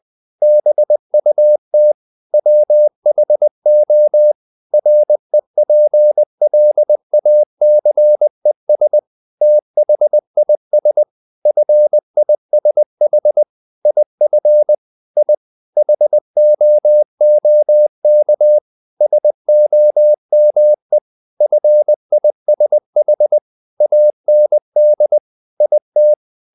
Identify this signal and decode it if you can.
Morse code, heard as a keyed tone, about 20 wpm, 600 Hz. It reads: BUT WHO REPLACES THIS FISH IF I HOOK SOME FISH AND IT
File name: the old man and the sea / the old man and the sea part 1 20wpm